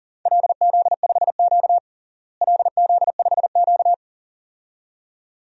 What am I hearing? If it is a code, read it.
Morse code, heard as a keyed tone, about 40 wpm, 700 Hz. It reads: LZ5Q LZ5Q
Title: DXCC-100 Call Signs - 4 Characters - Repeated 2x - 40wpm